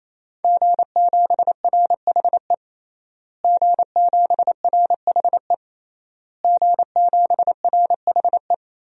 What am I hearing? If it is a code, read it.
Morse code, heard as a keyed tone, about 28 wpm, 700 Hz. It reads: G7RHE G7RHE G7RHE
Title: DXCC-100 Call Signs - 5 Characters - Repeated 3x - 28wpm